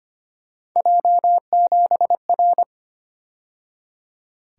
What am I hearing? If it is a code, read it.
Morse code, heard as a keyed tone, about 29 wpm, 700 Hz. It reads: J7R
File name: Speed-Racing - Three Letter-Number Combinations - 25wpm